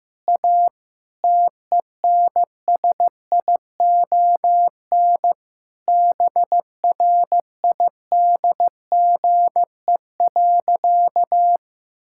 Morse code, heard as a keyed tone, about 16 wpm, 700 Hz. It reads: A TENSION BRIDGE.
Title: Sentences from Top 1600 Words 15wpm